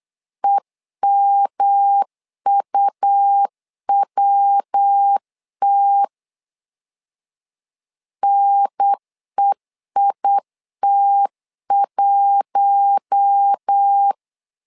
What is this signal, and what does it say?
Morse code, heard as a keyed tone, about 8 wpm, 800 Hz. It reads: EMUWT NEIT1
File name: LEZIONI / 4 Q1 8wpm 3035